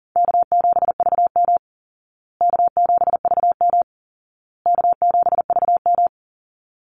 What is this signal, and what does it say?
Morse code, heard as a keyed tone, about 37 wpm, 700 Hz. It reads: K7VM K7VM K7VM
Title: Call Signs Repeated 2x - 40wpm